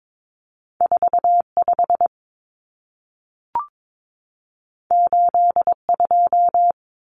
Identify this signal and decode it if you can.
Morse code, heard as a keyed tone, about 22 wpm, 700 Hz. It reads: 45 82
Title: Common Cut Numbers - 2-Digit 22wpm